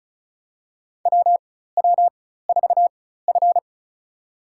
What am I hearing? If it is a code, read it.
Morse code, heard as a keyed tone, about 34 wpm, spaced as for 16 wpm, 700 Hz. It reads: WW4F
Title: USA Call Signs - 4x Character Spacing - 35wpm